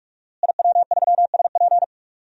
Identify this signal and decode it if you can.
Morse code, heard as a keyed tone, about 47 wpm, 700 Hz. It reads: IW3SP
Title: DXCC-100 Call Signs - 5 Characters - Repeated 2x - 45wpm